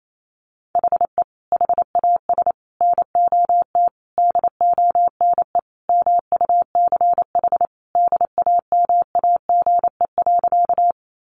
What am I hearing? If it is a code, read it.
Morse code, heard as a keyed tone, about 28 wpm, 700 Hz. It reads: HE HAS NOT DONE MUCH DAMAGE.
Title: Sentences from Top 1100 Words 28wpm